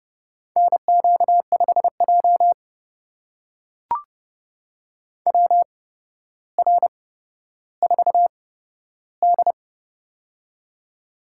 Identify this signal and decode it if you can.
Morse code, heard as a keyed tone, about 32 wpm, spaced as for 13 wpm, 700 Hz. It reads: NQ5J WR4D
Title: USA Call Signs - 4 Characters - 8x Character Spacing - 30wpm